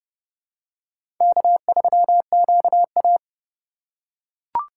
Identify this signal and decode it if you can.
Morse code, heard as a keyed tone, about 31 wpm, 700 Hz. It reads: K3QA
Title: USA Call Signs - 4 Characters - 2x Character Spacing - 30wpm